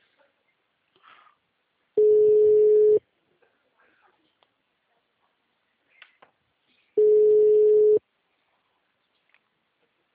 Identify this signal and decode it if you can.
Ringback tone (Europe, 425 Hz), with speech.